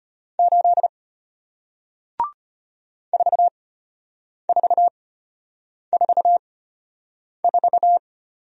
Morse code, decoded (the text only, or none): 8 4444